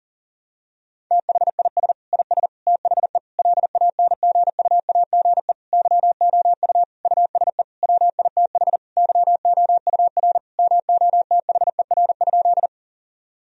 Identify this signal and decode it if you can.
Morse code, heard as a keyed tone, about 40 wpm, 700 Hz. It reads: THIS IS THE LANGUAGE YOU USE WITH YOUR MOTHER?
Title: Sentences from Top 1000 Words - Review Part 3 - 40wpm